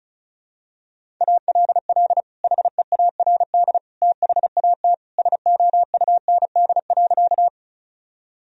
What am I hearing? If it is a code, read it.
Morse code, heard as a keyed tone, about 34 wpm, 700 Hz. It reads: ALL HEARD THAT SOUND.
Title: Sentences from Top 1000 Words - Review Part 1 - 35wpm